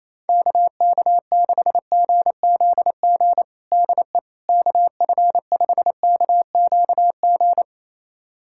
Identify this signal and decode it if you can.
Morse code, heard as a keyed tone, about 28 wpm, 700 Hz. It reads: KK6GZG DE KF5KQG
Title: USA Callsign DE Callsign - 6 Characters - 28wpm